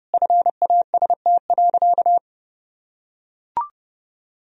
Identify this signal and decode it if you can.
Morse code, heard as a keyed tone, about 30 wpm, 700 Hz. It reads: FAST.
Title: Sentences from Top 1000 Words - Review Part 2 - 30wpm